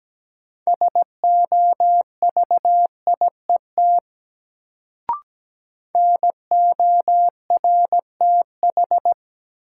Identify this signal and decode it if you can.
Morse code, heard as a keyed tone, about 17 wpm, 700 Hz. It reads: SOVIET NORTH
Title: Sets of 5 Words Encore - New Word Drill - 17wpm